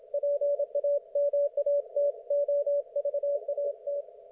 Morse code, heard as a keyed tone, about 27 wpm, 560 Hz. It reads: PAMATOVAT
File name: Archiv / CUC179 / ok1fhi cul179